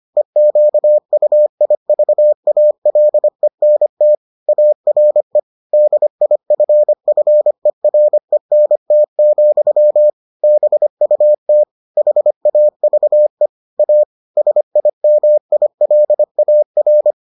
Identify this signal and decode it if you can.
Morse code, heard as a keyed tone, about 25 wpm, 600 Hz. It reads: EQUIVALENT ARE DIFFERENT, BUT HAVE A SIMILAR